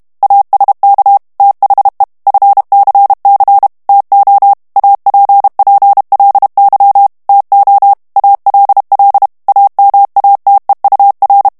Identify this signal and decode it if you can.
Morse code, heard as a keyed tone, about 32 wpm, 800 Hz. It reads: ASK THE FCC TO APPLY TO ALL AMATEUR
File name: morse mp3 / ANGLAIS texte clair 3 vitesse 30 mots minute